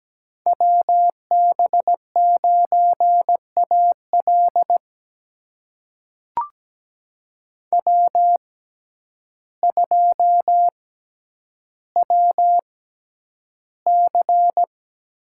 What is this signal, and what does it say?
Morse code, heard as a keyed tone, about 17 wpm, spaced as for 9 wpm, 700 Hz. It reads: WB9AL W2WC
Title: USA Call Signs - 6x Character Spacing - 17wpm